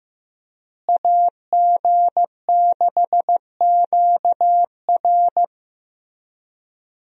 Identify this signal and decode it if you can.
Morse code, heard as a keyed tone, about 15 wpm, 700 Hz. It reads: AG6QR
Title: USA Call Signs - 5 Characters - Repeated 3x - 15wpm